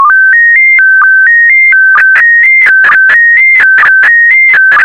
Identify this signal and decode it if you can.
SSTV (slow-scan television), PD 290.
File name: sstv / PD290-VK4MSL